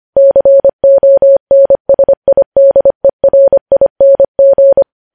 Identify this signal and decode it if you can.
Morse code, heard as a keyed tone, about 25 wpm, 550 Hz. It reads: CONSIDERING
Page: News Headlines in Morse Code at 25 WPM